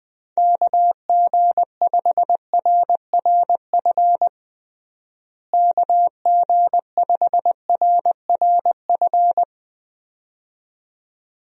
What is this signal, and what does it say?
Morse code, heard as a keyed tone, about 21 wpm, 700 Hz. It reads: KG5RRF KG5RRF
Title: USA Call Signs - 6 Characters - Repeated 2x - 20wpm